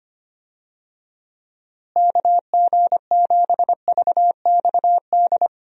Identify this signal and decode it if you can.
Morse code, heard as a keyed tone, about 25 wpm, 700 Hz. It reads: KG7VXD
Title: USA Call Signs - 6 Characters - Repeated 3x - 25wpm